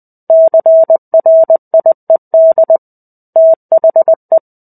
Morse code, heard as a keyed tone, about 20 wpm, 650 Hz. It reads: CRIED THE
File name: 05-Chapter-05 20 WPM